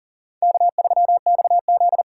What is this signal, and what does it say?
Morse code, heard as a keyed tone, about 40 wpm, 700 Hz. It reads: K3XZ
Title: USA Call Signs - 4 Characters - 3x Character Spacing - 40wpm